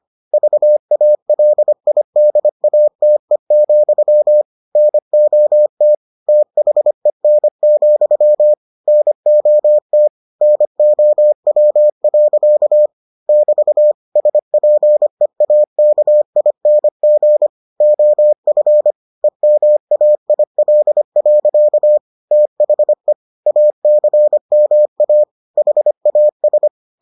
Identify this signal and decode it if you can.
Morse code, heard as a keyed tone, about 25 wpm, 600 Hz. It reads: VALIDATE, NOT THEN, NOT NOW. <BT> SPEAKING OF EMAIL. THE ACMA HAS